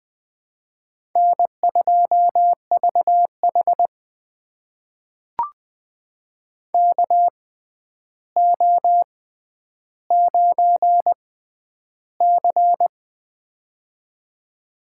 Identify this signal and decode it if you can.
Morse code, heard as a keyed tone, about 20 wpm, spaced as for 10 wpm, 700 Hz. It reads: N2VH KO9C
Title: USA Call Signs - 4 Characters - 6x Character Spacing - 20wpm